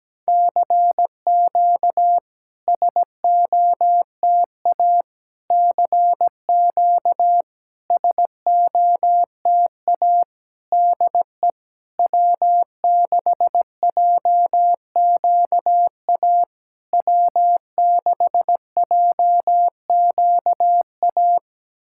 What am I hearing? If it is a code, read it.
Morse code, heard as a keyed tone, about 17 wpm, 700 Hz. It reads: CQ SOTA CQ SOTA DE W6JQA W6JQA
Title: SOTA QSO Practice 17wpm